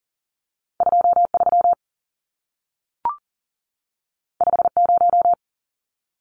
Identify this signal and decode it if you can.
Morse code, heard as a keyed tone, about 40 wpm, 700 Hz. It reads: 23 50
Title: Numbers - 2-Digit 40wpm